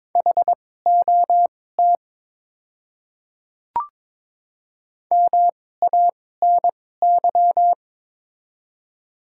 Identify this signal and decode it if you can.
Morse code, heard as a keyed tone, about 22 wpm, spaced as for 16 wpm, 700 Hz. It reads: HOT MANY
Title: 2x Character Spacing - Top 300 Words - 22wpm